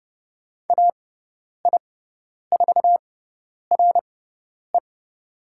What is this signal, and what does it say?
Morse code, heard as a keyed tone, about 30 wpm, spaced as for 10 wpm, 700 Hz. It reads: AI4RE
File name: Lesson 16g - Letter c - Call Signs 30wpm